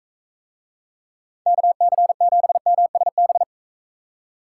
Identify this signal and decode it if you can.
Morse code, heard as a keyed tone, about 44 wpm, 700 Hz. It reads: KC7MSB